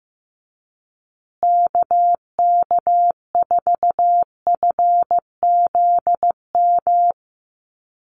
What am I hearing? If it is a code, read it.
Morse code, heard as a keyed tone, about 15 wpm, 700 Hz. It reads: KK4FZM
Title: USA Call Signs Repeated 2x 15wpm